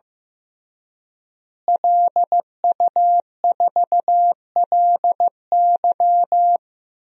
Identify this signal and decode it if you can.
Morse code, heard as a keyed tone, about 16 wpm, 700 Hz. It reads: LU4LY
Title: DXCC-100 Call Signs - 5 Characters - Repeated 3x - 15wpm